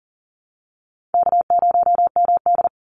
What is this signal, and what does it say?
Morse code, heard as a keyed tone, about 40 wpm, 700 Hz. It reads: K0MD
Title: Call Signs Repeated 2x - 40wpm